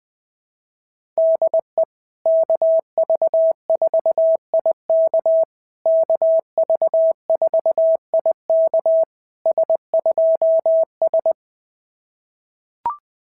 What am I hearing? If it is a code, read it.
Morse code, heard as a keyed tone, about 20 wpm, 650 Hz. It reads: DE KV4IK KV4IK S2S
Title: SOTA QSO Practice 20wpm